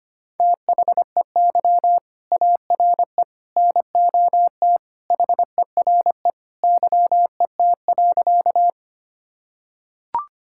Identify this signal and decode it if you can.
Morse code, heard as a keyed tone, about 25 wpm, 700 Hz. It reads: THEY ARE NOT HERE YET.
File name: practice / 25wpm / Sentences with 2-Word Phrases - 25wpm